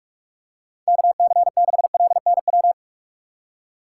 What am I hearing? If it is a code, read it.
Morse code, heard as a keyed tone, about 45 wpm, 700 Hz. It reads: KC6LNW
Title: USA Call Signs - 6 Characters - 2x Character Spacing - 45wpm